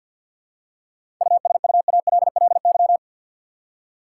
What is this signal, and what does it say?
Morse code, heard as a keyed tone, about 53 wpm, 700 Hz. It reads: USUALLY
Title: Seven Letter Words - 50wpm